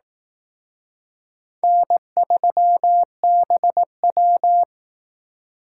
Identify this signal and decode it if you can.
Morse code, heard as a keyed tone, about 19 wpm, 700 Hz. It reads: N3BW
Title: Speed Racing - USA 1x2 Call Signs 15wpm